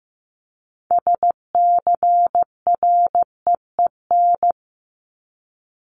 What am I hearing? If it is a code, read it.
Morse code, heard as a keyed tone, about 15 wpm, 700 Hz. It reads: SCREEN
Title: Top 751-1000 Words 15wpm